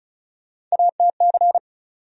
Morse code, heard as a keyed tone, about 36 wpm, 700 Hz. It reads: ATC
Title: USA Call Sign Postfixes - 3 Letters 35wpm